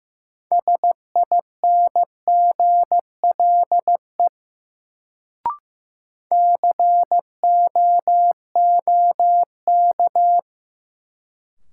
Morse code, heard as a keyed tone, about 15 wpm, 700 Hz. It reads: SINGLE COOK
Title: Sentences from Top 700 Words - New Word Drill - 15wpm